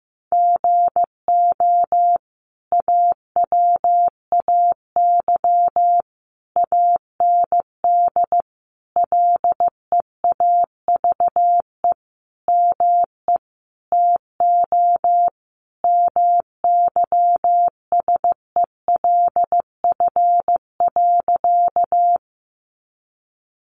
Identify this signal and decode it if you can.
Morse code, heard as a keyed tone, about 15 wpm, 700 Hz. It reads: GO AWAY AND LEAVE ME TO MYSELF.
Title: Sentences from Top 1300 Words 15wpm